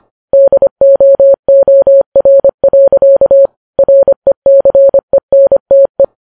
Morse code, heard as a keyed tone, about 25 wpm, 550 Hz. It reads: DOOR. RECENTE